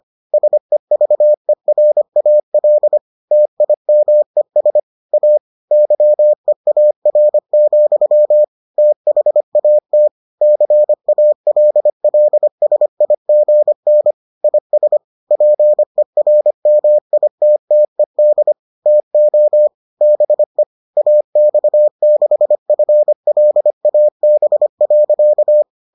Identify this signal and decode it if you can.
Morse code, heard as a keyed tone, about 25 wpm, 600 Hz. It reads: SEVERAL TIMES A YEAR, THAT CALLSIGN IS PERMITTED TO BE AX6FLAB.